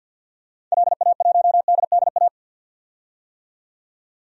Morse code, heard as a keyed tone, about 52 wpm, 700 Hz. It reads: LA1DDA
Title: DXCC-100 Call Signs - 6 Characters - 50wpm